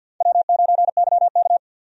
Morse code, heard as a keyed tone, about 49 wpm, 700 Hz. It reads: W9YK
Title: USA Call Signs - 4 Characters - Repeated 3x - 50wpm